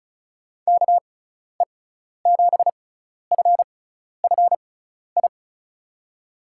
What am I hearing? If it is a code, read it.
Morse code, heard as a keyed tone, about 36 wpm, spaced as for 12 wpm, 700 Hz. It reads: KE7FFI
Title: USA Call Signs - 6 Characters - 6x Character Spacing - 35wpm